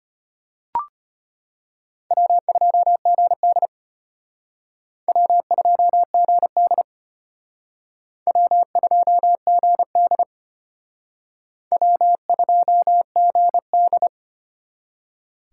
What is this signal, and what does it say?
Morse code, heard as a keyed tone, about 32 wpm, 700 Hz. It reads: W2GD W2GD W2GD W2GD